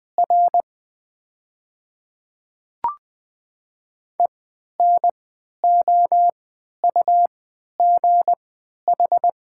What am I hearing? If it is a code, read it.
Morse code, heard as a keyed tone, about 20 wpm, spaced as for 11 wpm, 700 Hz. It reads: R ENOUGH
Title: 3x Character Spacing - Top 500 Words - 20wpm